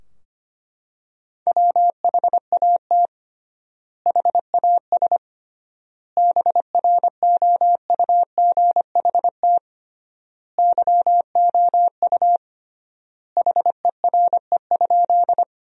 Morse code, heard as a keyed tone, about 25 wpm, 700 Hz. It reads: WHAT HAS BROUGHT YOU HERE?
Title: 3x Word-Spacing - Sentences from Top 500 Words - 25wpm